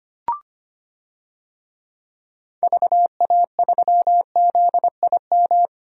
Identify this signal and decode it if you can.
Morse code, heard as a keyed tone, about 25 wpm, 700 Hz. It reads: VA3ZIM
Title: DXCC-100 Call Signs - 6 Characters - 25wpm